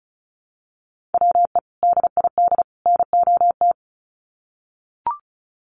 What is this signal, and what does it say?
Morse code, heard as a keyed tone, about 34 wpm, 700 Hz. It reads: WE DID NOT
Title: Sets of 3 Words 35wpm